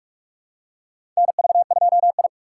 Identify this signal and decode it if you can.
Morse code, heard as a keyed tone, about 46 wpm, 700 Hz. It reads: NV2I
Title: USA Call Signs - 3x Character Spacing - 45wpm